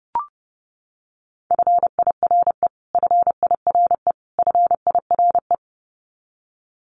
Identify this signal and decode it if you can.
Morse code, heard as a keyed tone, about 30 wpm, 700 Hz. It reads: FIRE FIRE FIRE
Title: Top 500 Words Repeated 30wpm